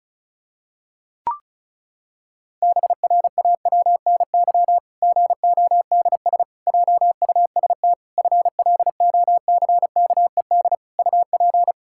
Morse code, heard as a keyed tone, about 36 wpm, 700 Hz. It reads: BRAWNY GODS JUST FLOCKED UP
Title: Pangram Challenge v2 - 35wpm